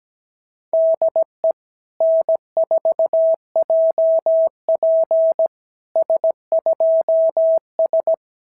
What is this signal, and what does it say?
Morse code, heard as a keyed tone, about 17 wpm, 650 Hz. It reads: DE N4JP S2S
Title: SOTA QSO Practice 17wpm